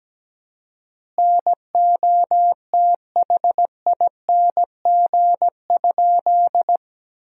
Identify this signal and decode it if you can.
Morse code, heard as a keyed tone, about 17 wpm, 700 Hz. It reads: NOTHING?
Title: 4x Word-Spacing - Sentences from Top 500 Words - 17wpm